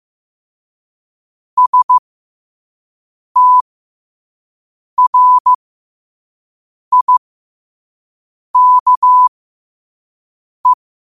Morse code, heard as a keyed tone, about 15 wpm, spaced as for 5 wpm, 1000 Hz. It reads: STRIKE
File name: qotdfiles / 5wpm